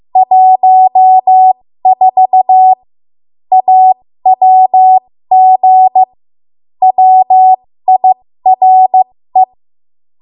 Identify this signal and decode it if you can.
Morse code, heard as a keyed tone, about 15 wpm, spaced as for 13 wpm, 750 Hz. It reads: 14 AWG WIRE
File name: Morse / 13WPM